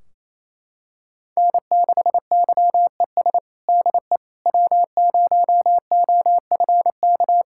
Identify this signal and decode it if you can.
Morse code, heard as a keyed tone, about 27 wpm, 700 Hz. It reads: N6YES DE W0OFK
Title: K1USN SST QSO Practice 28wpm